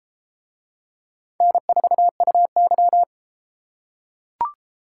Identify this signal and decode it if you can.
Morse code, heard as a keyed tone, about 31 wpm, 700 Hz. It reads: N4UY